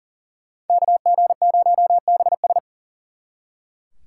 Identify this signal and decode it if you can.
Morse code, heard as a keyed tone, about 48 wpm, 700 Hz. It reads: KG0BS
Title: Speed Racing - USA 2x2 Call Signs 40wpm